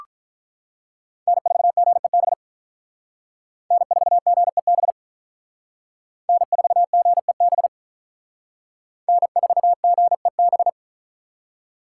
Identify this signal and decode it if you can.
Morse code, heard as a keyed tone, about 44 wpm, 700 Hz. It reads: N4GEB N4GEB N4GEB N4GEB